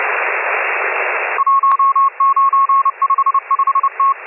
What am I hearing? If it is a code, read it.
Morse code, heard as a keyed tone, about 30 wpm, 1090 Hz, with radio noise. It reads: 1955T
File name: M12- -1010utc